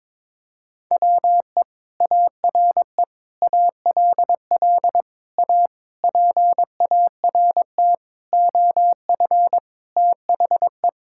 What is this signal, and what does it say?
Morse code, heard as a keyed tone, about 22 wpm, 700 Hz. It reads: WE ARE ALL A PART OF THE
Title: Sentences from Top 1000 Words - Review Part 6 - 22wpm